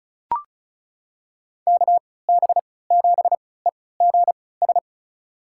Morse code, heard as a keyed tone, about 34 wpm, spaced as for 20 wpm, 700 Hz. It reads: KB7EGS
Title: USA Call Signs - 6 Characters - 3x Character Spacing - 35wpm